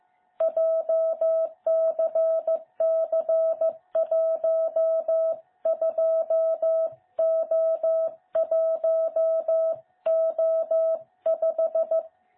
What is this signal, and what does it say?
Morse code, heard as a keyed tone, about 14 wpm, 660 Hz. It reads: JCC12O1O5
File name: jh8cbh / 2021cw